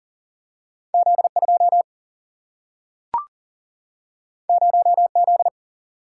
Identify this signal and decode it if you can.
Morse code, heard as a keyed tone, about 40 wpm, 700 Hz. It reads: Z 2 0 Z
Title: Single-Letter Single-Number 40wpm